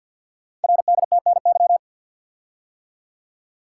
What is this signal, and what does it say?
Morse code, heard as a keyed tone, about 50 wpm, 690 Hz. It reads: RDTNY
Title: Code Groups - 5 Letters - 50wpm